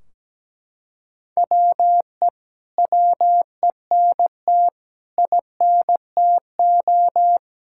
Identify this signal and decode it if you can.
Morse code, heard as a keyed tone, about 17 wpm, 700 Hz. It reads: WE WENT INTO
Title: Sentences from Top 1600 Words 17wpm